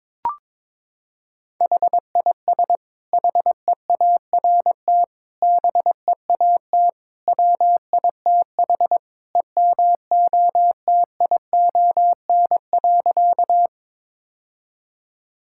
Morse code, heard as a keyed tone, about 22 wpm, 700 Hz. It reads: HIS HEART BEAT WITH EMOTION.
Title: Sentences from Top 1600 Words 22wpm